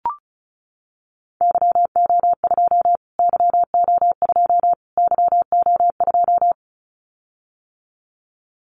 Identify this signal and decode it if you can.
Morse code, heard as a keyed tone, about 34 wpm, 700 Hz. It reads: YO2 YO2 YO2
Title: Callsign Prefixes for Top 25 DXCC - Repeated 3x - 35wpm